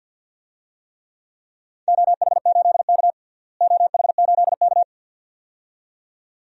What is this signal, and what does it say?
Morse code, heard as a keyed tone, about 47 wpm, 700 Hz. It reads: OH8K OH8K
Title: DXCC-100 Call Signs - 4 Characters - Repeated 2x - 50wpm